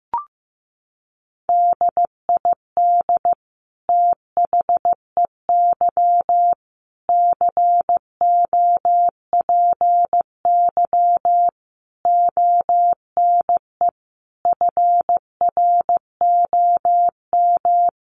Morse code, heard as a keyed tone, about 15 wpm, 700 Hz. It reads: DID THEY COPY ONE FROM
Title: Sentences from Top 1400 Words 15wpm